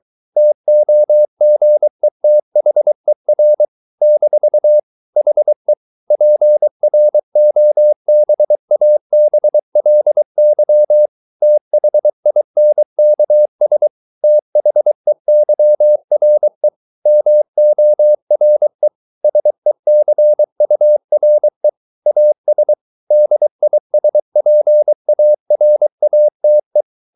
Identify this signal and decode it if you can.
Morse code, heard as a keyed tone, about 23 wpm, 600 Hz. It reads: TOGETHER - HE PROBABLY THINKS THEYRE MORE SECURE AS DISPARATE